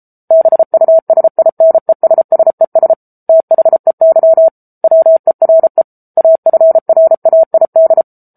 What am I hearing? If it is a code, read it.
Morse code, heard as a keyed tone, about 34 wpm, 650 Hz. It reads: BUSINESSES THEY WERE AFRAID
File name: Book Two / 07-Chapter 7 - 33 WPM